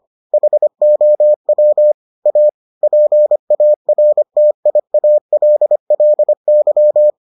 Morse code, heard as a keyed tone, about 25 wpm, 600 Hz. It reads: HOW A PARTIALLY